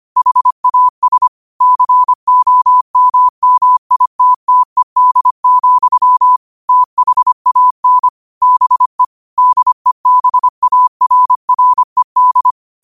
Morse code, heard as a keyed tone, about 25 wpm, 1000 Hz. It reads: HAS COMMITTED, THAN BE DEBARRED